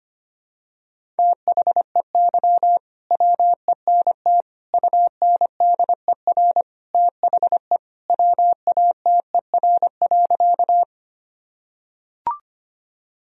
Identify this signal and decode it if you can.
Morse code, heard as a keyed tone, about 25 wpm, 700 Hz. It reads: THEY WENT UNDER THE WATER.
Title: Sentences from Top 1000 Words - Review Part 1 - 25wpm